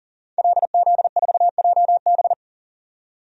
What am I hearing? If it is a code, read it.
Morse code, heard as a keyed tone, about 40 wpm, 700 Hz. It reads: LZ4JB
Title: DXCC-100 Call Signs - 5 Characters - 40wpm